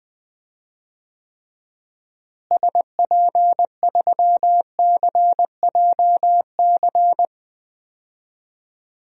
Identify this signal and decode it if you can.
Morse code, heard as a keyed tone, about 20 wpm, 700 Hz. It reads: SP3CJC
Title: DXCC-100 Call Signs - 6 Characters - Repeated 3x - 20wpm